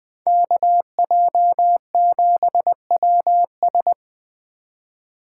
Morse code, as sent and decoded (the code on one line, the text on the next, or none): -.- .--- --... .-- ...
KJ7WS